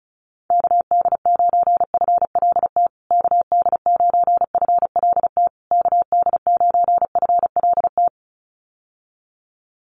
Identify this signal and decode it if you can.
Morse code, heard as a keyed tone, about 34 wpm, 700 Hz. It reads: KD9FLT KD9FLT KD9FLT
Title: USA Call Signs Repeated 3x 35wpm